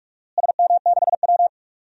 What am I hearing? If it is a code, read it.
Morse code, heard as a keyed tone, about 46 wpm, 700 Hz. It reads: SM6W
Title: DXCC-100 Call Signs - 4 Characters - Repeated 3x - 45wpm